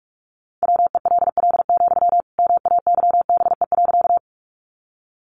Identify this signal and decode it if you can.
Morse code, heard as a keyed tone, about 44 wpm, 700 Hz. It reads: WELL, MAYBE.
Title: Sentences from Top 1000 Words 45wpm